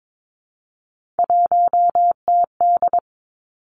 Morse code, heard as a keyed tone, about 22 wpm, 700 Hz. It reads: 1TD
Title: Three Challenging Letter-Number Combinations 22wpm